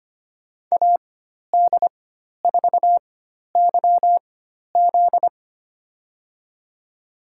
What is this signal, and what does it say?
Morse code, heard as a keyed tone, about 25 wpm, spaced as for 12 wpm, 700 Hz. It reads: AD4YZ
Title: USA Call Signs - 5 Characters - 4x Character Spacing - 25wpm